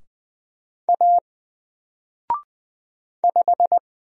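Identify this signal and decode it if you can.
Morse code, heard as a keyed tone, about 20 wpm, 700 Hz. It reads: A 5
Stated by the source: Single letter and number Morse drill at 20 wpm, from N4LCD.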